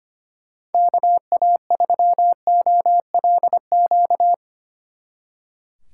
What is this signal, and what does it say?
Morse code, heard as a keyed tone, about 32 wpm, 700 Hz. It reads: KA3OLQ